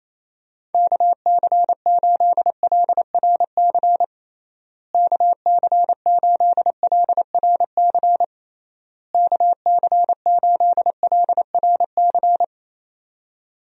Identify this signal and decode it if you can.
Morse code, heard as a keyed tone, about 28 wpm, 700 Hz. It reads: KC8LRC KC8LRC KC8LRC
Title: USA Call Signs - 6 Characters - Repeated 3x - 28wpm